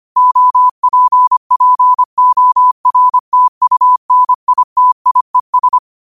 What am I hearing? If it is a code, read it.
Morse code, heard as a keyed tone, about 25 wpm, 1000 Hz. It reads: OPPORTUNITIES
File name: qotdfiles / 25wpm